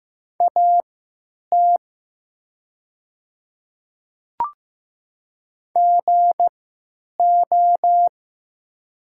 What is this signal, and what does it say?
Morse code, heard as a keyed tone, about 15 wpm, spaced as for 9 wpm, 700 Hz. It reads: AT GO
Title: 3x Character Spacing - Top 100 Words - 15wpm